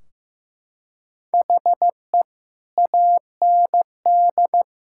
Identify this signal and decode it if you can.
Morse code, heard as a keyed tone, about 15 wpm, 700 Hz. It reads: HE AND
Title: Sentences from Top 1600 Words 15wpm